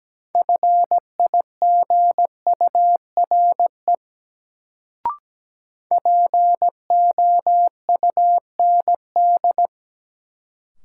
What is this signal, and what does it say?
Morse code, heard as a keyed tone, about 17 wpm, 700 Hz. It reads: FIGURE POUND
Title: Sentences from Top 1000 Words - Review Part 2 - New Word Drill - 17wpm